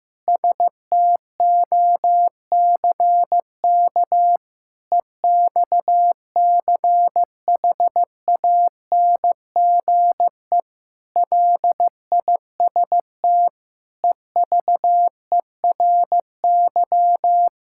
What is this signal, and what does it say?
Morse code, heard as a keyed tone, about 15 wpm, 700 Hz. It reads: STOCK EXCHANGE LIST EVERY
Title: Sentences from Top 1000 Words - Review Part 6 - 15wpm